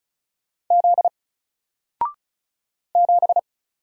Morse code, heard as a keyed tone, about 34 wpm, 700 Hz. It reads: Z 7
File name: Single-Letter-Number-35wpm